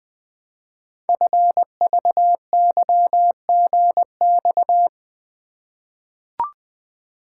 Morse code, heard as a keyed tone, about 20 wpm, 700 Hz. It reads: FVYGX
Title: Code Groups - 5 Letters - 20wpm